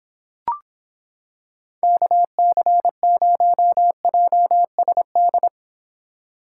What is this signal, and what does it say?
Morse code, heard as a keyed tone, about 26 wpm, 700 Hz. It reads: KC0JSD